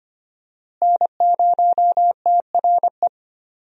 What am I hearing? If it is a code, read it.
Morse code, heard as a keyed tone, about 25 wpm, 700 Hz. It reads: N0TRE
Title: USA Call Signs - 5 Characters - Repeated 2x - 25wpm